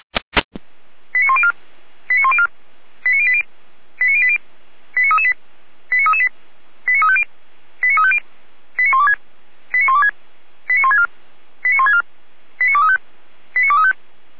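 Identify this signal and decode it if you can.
Five-tone selective call (ZVEI), heard as quick runs of five tones, tones of about 70 ms. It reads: 89104 89104 89080 89080 89338 89338 89350 89350 89126 89126 89164 89164 89235 89235
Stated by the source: ZVEI five-tone call 89280 received on an SDR.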